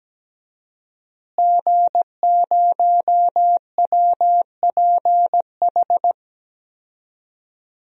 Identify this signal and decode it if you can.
Morse code, heard as a keyed tone, about 17 wpm, 700 Hz. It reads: G0WPH
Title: DXCC-100 Call Signs - 5 Characters - Repeated 3x - 17wpm